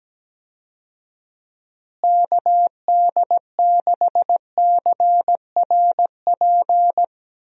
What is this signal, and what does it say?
Morse code, heard as a keyed tone, about 17 wpm, 700 Hz. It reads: KD6CRP
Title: USA Call Signs - 6 Characters - Repeated 3x - 17wpm